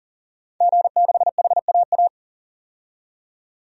Morse code, heard as a keyed tone, about 40 wpm, 700 Hz. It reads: G6HAA
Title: DXCC-100 Call Signs - 5 Characters - 40wpm